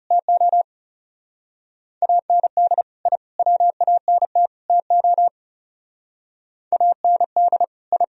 Morse code, heard as a keyed tone, about 37 wpm, 700 Hz. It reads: TO AND I WANT TO AND I
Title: Speed-Racing - Sets of 4 Words 30wpm